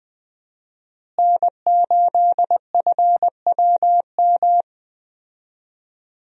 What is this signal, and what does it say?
Morse code, heard as a keyed tone, about 20 wpm, 700 Hz. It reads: N8FWM
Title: USA Call Signs - 5 Characters - Repeated 2x - 20wpm